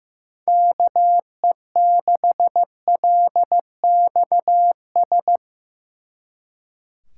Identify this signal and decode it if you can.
Morse code, heard as a keyed tone, about 17 wpm, 700 Hz. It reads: KE6LXS
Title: Speed Racing - USA 2x3 Call Signs 15wpm